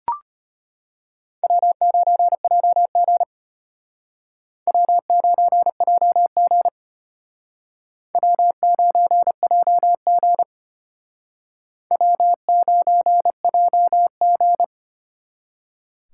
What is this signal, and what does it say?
Morse code, heard as a keyed tone, about 34 wpm, 700 Hz. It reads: W9JG W9JG W9JG W9JG